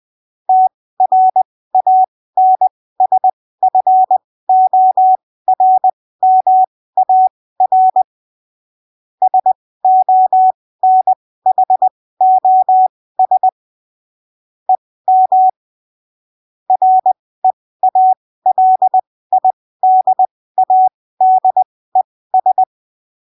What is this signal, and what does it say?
Morse code, heard as a keyed tone, about 20 wpm, spaced as for 15 wpm, 750 Hz. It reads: TRANSFORMAR SONHOS EM REALIDADES